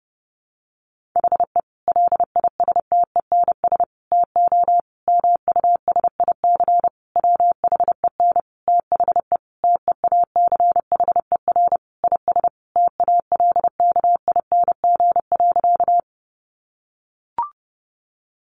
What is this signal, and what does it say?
Morse code, heard as a keyed tone, about 30 wpm, 700 Hz. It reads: HE LISTENS TO MUSIC WHEN THE TEACHER IS TALKING.
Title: Additional Easy Phrases 30wpm